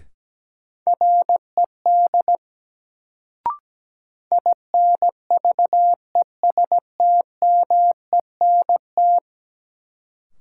Morse code, heard as a keyed tone, about 17 wpm, 700 Hz. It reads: RED INVESTMENT
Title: Sentences from Top 700 Words - New Word Drill - 17wpm